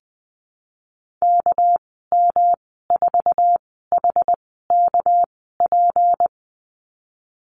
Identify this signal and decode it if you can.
Morse code, heard as a keyed tone, about 20 wpm, spaced as for 15 wpm, 700 Hz. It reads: KM4HKP
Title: 200 Lesson 32g - Letter k - Call Signs 20wpm